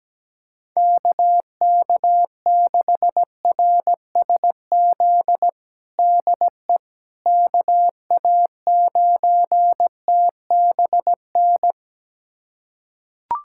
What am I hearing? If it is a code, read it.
Morse code, heard as a keyed tone, about 17 wpm, 700 Hz. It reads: KK6RSZ DE KA9TBN
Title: USA Callsign DE Callsign 17wpm